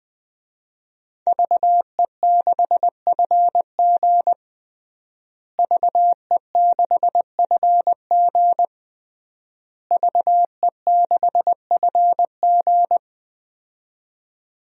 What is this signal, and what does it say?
Morse code, heard as a keyed tone, about 21 wpm, 700 Hz. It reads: VE6FG VE6FG VE6FG
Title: DXCC-100 Call Signs - 5 Characters - Repeated 3x - 20wpm